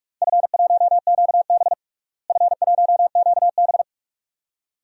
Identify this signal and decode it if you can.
Morse code, heard as a keyed tone, about 46 wpm, 700 Hz. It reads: F1QB F1QB
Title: DXCC-100 Call Signs - 4 Characters - Repeated 2x - 45wpm